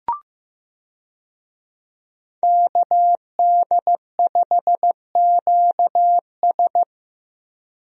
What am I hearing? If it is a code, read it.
Morse code, heard as a keyed tone, about 15 wpm, 700 Hz. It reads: KD5QS
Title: USA Call Signs - 5 Characters - Repeated 2x - 15wpm